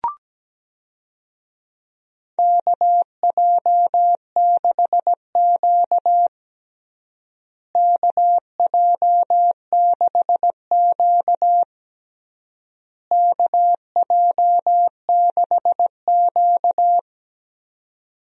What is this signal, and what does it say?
Morse code, heard as a keyed tone, about 17 wpm, 700 Hz. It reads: KJ6Q KJ6Q KJ6Q
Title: USA Call Signs - 4 Characters - Repeated 3x - 17wpm